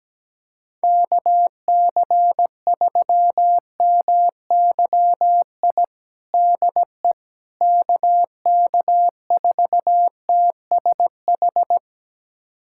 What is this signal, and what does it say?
Morse code, heard as a keyed tone, about 17 wpm, 700 Hz. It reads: KC3MYI DE KK4TSH
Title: USA Callsign DE Callsign - 6 Characters - 17wpm